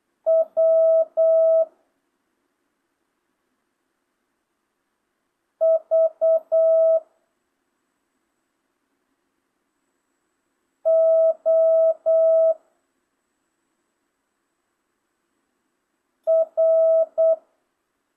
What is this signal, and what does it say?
Morse code, heard as a keyed tone, about 8 wpm, 650 Hz. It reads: W V O R
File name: jh8cbh / 2025cw